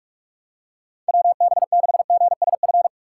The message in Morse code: WB6GSF

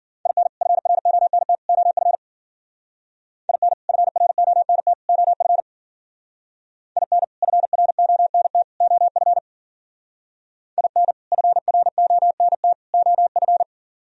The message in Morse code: IN FRONT OF IN FRONT OF IN FRONT OF IN FRONT OF

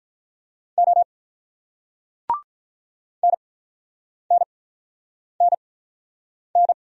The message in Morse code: O NNNN

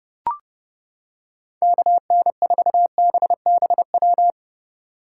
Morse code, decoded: KN4BBW

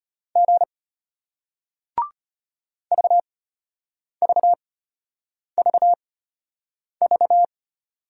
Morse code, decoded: G VVVV

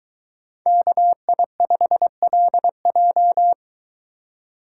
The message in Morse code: KI5LJ

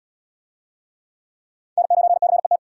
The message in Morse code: N9CET

